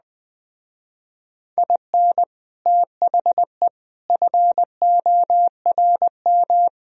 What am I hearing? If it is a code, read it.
Morse code, heard as a keyed tone, about 20 wpm, 700 Hz. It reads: IN THE FORM
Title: Speed-Racing - Sets of 4 Words 17wpm